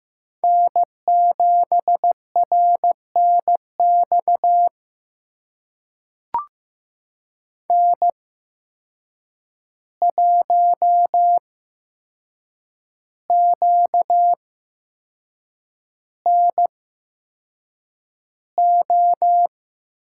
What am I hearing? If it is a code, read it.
Morse code, heard as a keyed tone, about 16 wpm, spaced as for 6 wpm, 700 Hz. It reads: N7RNX N1QNO
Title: USA Call Signs - 5 Characters - 8x Character Spacing - 15wpm